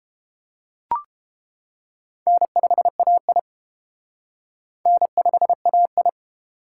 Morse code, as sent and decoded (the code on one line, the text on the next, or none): -. ..... .- .. / -. ..... .- ..
N5AI N5AI